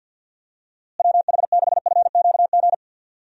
Morse code, heard as a keyed tone, about 51 wpm, 700 Hz. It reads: WH6FQG
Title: USA Call Signs - 6 Characters - Repeated 3x - 50wpm